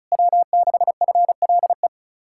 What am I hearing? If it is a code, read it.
Morse code, heard as a keyed tone, about 44 wpm, 700 Hz. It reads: W6FLE